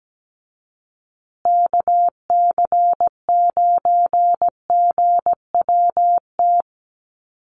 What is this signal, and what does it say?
Morse code, heard as a keyed tone, about 17 wpm, 700 Hz. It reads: KC9GWT
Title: USA Call Signs Repeated 3x 17wpm